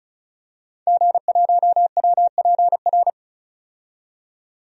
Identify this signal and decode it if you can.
Morse code, heard as a keyed tone, about 34 wpm, 700 Hz. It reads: G1WPR
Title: DXCC-100 Call Signs - 5 Characters - 35wpm